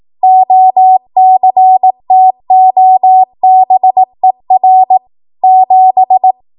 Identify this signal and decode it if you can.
Morse code, heard as a keyed tone, about 18 wpm, 750 Hz. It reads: OCTOBER 7
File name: Archive / 18 WPM / 18WPM